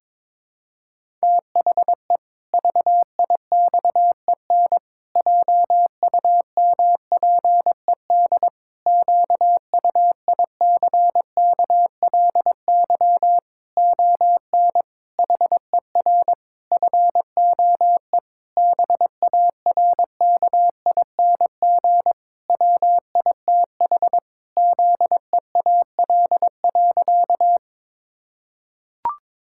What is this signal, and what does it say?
Morse code, heard as a keyed tone, about 22 wpm, 700 Hz. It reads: THE VIXEN JUMPED QUICKLY ON HER FOE BARKING WITH ZEAL.
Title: Pangram Challenge v2 - 22wpm